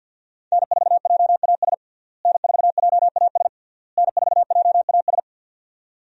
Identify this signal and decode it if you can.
Morse code, heard as a keyed tone, about 51 wpm, 700 Hz. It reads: N4JAS N4JAS N4JAS
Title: USA Call Signs - 5 Characters - Repeated 3x - 50wpm